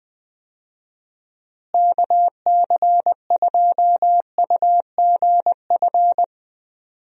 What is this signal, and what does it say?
Morse code, heard as a keyed tone, about 20 wpm, 700 Hz. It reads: KC2UGF
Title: USA Call Signs - 6 Characters - Repeated 2x - 20wpm